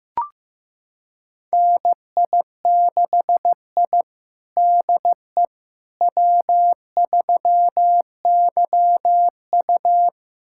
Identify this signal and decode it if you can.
Morse code, heard as a keyed tone, about 15 wpm, 700 Hz. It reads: NI6I DE W3YU
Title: USA Callsign DE Callsign - 4 Characters - 15wpm